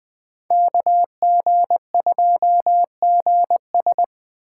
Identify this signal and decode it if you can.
Morse code, heard as a keyed tone, about 21 wpm, 700 Hz. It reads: KG2GS
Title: Speed Racing - USA 2x2 Call Signs 17wpm